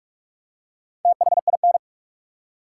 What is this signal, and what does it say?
Morse code, heard as a keyed tone, about 47 wpm, 700 Hz. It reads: THIN
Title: Four Letter Words - 45wpm